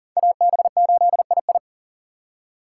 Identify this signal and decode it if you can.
Morse code, heard as a keyed tone, about 52 wpm, 700 Hz. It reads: AB8II